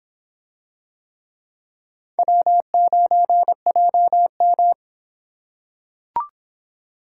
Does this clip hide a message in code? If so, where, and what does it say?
Yes: Morse code from 2.2 to 4.7 s, heard as a keyed tone, about 25 wpm, 700 Hz. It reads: W9JM